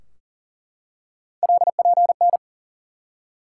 Morse code, heard as a keyed tone, about 40 wpm, 690 Hz. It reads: LPN
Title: Practice QSO - Exchange 3 - More Info - 40wpm